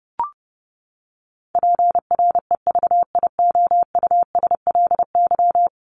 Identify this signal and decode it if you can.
Morse code, heard as a keyed tone, about 30 wpm, 700 Hz. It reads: PREVIOUSLY
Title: LY Suffix - Top 100 Words with Base Words 30wpm